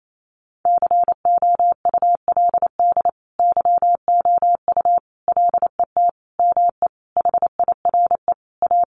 Morse code, heard as a keyed tone, about 28 wpm, 700 Hz. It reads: COULD YOU LET ME HIRE A
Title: Sentences from Top 1400 Words 28wpm